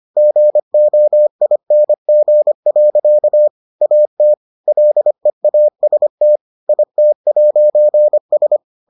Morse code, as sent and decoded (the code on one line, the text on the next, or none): --. --- .. -. --. .-.-.- / .- - / .-.. .- ... - / .. - .----. ...
GOING. AT LEAST IT'S